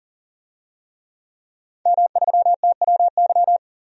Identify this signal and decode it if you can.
Morse code, heard as a keyed tone, about 40 wpm, 700 Hz. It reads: M3TWY
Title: DXCC-100 Call Signs - 5 Characters - 40wpm